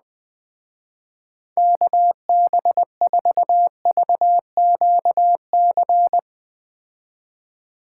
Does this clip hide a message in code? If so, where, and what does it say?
Yes: Morse code from 1.6 to 6.2 s, heard as a keyed tone, about 21 wpm, 700 Hz. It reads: KB4VQC